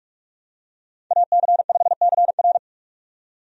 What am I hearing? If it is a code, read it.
Morse code, heard as a keyed tone, about 46 wpm, 700 Hz. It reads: AC5CR
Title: USA Call Signs - 5 Characters - 45wpm